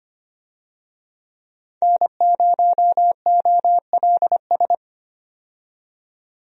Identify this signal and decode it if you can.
Morse code, heard as a keyed tone, about 25 wpm, 690 Hz. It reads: N0OLS
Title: USA Call Signs - 5 Characters - 25wpm